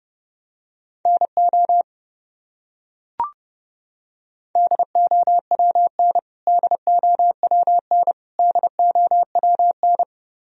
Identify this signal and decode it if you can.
Morse code, heard as a keyed tone, about 32 wpm, 700 Hz. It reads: NO DOWN DOWN DOWN
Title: Top 100 Words Repeated 30wpm